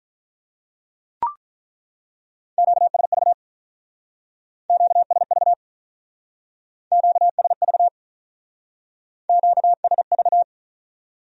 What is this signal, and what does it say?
Morse code, heard as a keyed tone, about 43 wpm, 700 Hz. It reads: QSV QSV QSV QSV
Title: Speed-Racing Q-Codes - 35wpm